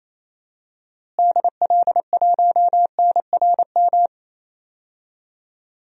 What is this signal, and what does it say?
Morse code, heard as a keyed tone, about 28 wpm, 700 Hz. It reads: DL1NRM
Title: DXCC-100 Call Signs - 6 Characters - Repeated 2x - 28wpm